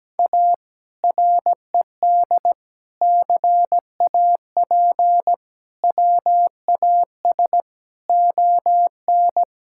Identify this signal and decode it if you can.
Morse code, heard as a keyed tone, about 17 wpm, 700 Hz. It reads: A RED CAP WAS ON
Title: Sentences from Top 1600 Words 17wpm